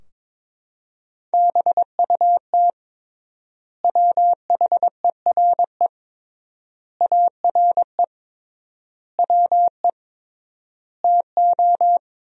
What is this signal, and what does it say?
Morse code, heard as a keyed tone, about 22 wpm, 700 Hz. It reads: BUT WHERE ARE WE TO
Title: 3x Word-Spacing - Sentences from Top 200 Words - 22wpm